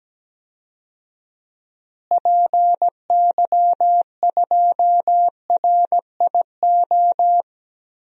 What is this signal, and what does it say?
Morse code, heard as a keyed tone, about 17 wpm, 700 Hz. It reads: PY2RIO
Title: DXCC-100 Call Signs - 6 Characters - Repeated 3x - 17wpm